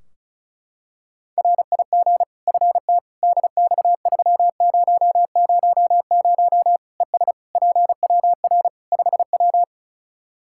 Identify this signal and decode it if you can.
Morse code, heard as a keyed tone, about 36 wpm, 700 Hz. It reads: RIG FT DX3000 ES PWR 5W
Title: K1USN SST QSO Practice 35wpm